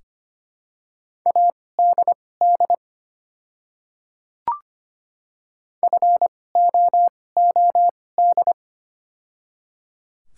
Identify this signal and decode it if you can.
Morse code, heard as a keyed tone, about 25 wpm, spaced as for 18 wpm, 700 Hz. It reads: ADD FOOD
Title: 2x Character Spacing - Top 300 Words - 25wpm